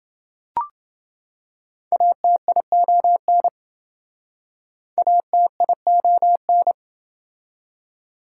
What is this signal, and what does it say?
Morse code, heard as a keyed tone, about 27 wpm, 700 Hz. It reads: ATION ATION